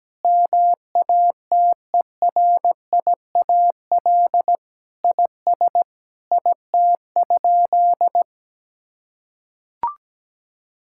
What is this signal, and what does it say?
Morse code, heard as a keyed tone, about 17 wpm, 700 Hz. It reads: MATERIAL IS IT?
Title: Sentences from Top 1000 Words - Review Part 5 - 17wpm